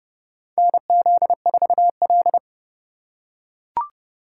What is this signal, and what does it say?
Morse code, heard as a keyed tone, about 30 wpm, 700 Hz. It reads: NZ4L
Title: USA Call Signs - 4 Characters - 6x Character Spacing - 30wpm